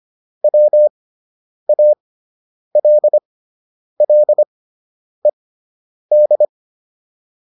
Morse code, heard as a keyed tone, about 25 wpm, spaced as for 9 wpm, 600 Hz. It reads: WALLED